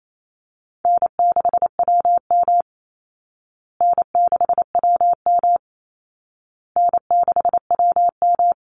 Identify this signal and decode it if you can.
Morse code, heard as a keyed tone, about 28 wpm, 700 Hz. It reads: N6WM N6WM N6WM
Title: Call Signs Repeated 2x - 28wpm